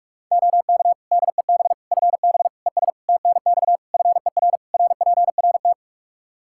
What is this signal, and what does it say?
Morse code, heard as a keyed tone, about 42 wpm, 700 Hz. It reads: OK DEB FB ES TNX FER RPRT
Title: Practice QSO - Exchange 2 - Info - 45wpm